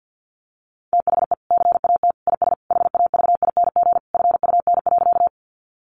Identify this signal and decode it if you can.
Morse code, heard as a keyed tone, about 50 wpm, 700 Hz. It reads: THE CAT IS HAVING FUN.
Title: Easy Phrases 50wpm